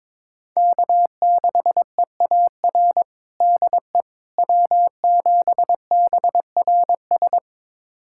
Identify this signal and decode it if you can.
Morse code, heard as a keyed tone, about 22 wpm, 700 Hz. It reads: K6EAR DE W7BRS
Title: USA Callsign DE Callsign - 5 Characters - 22wpm